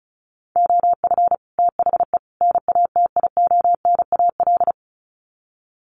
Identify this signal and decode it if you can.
Morse code, heard as a keyed tone, about 34 wpm, 700 Hz. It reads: OF THE NATIONAL
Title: Sets of 3 Words 35wpm